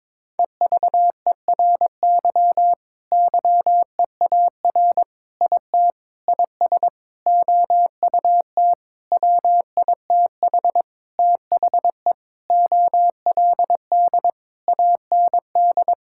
Morse code, heard as a keyed tone, about 22 wpm, 700 Hz. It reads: EVERY YEAR IT IS OUT WITH THE OLD AND
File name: Sentences with 9-Word Phrases - 22wpm